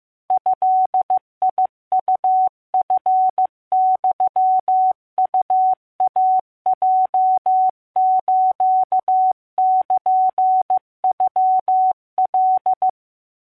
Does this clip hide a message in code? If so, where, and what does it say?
Yes: Morse code (Japanese Wabun code) from 0.3 to 12.9 s, heard as a keyed tone, about 15 wpm, 750 Hz. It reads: ドウチユウイヲスルノカ